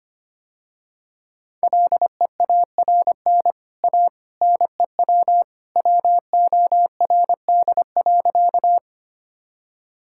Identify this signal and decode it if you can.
Morse code, heard as a keyed tone, about 25 wpm, 700 Hz. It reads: LEARN A NEW WORD.
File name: Easy Sentences with Hints - 25wpm